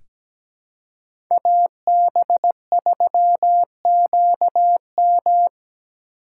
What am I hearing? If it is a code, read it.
Morse code, heard as a keyed tone, about 17 wpm, 700 Hz. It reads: AB3QM
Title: USA Call Signs - 5 Characters - 3x Character Spacing - 17wpm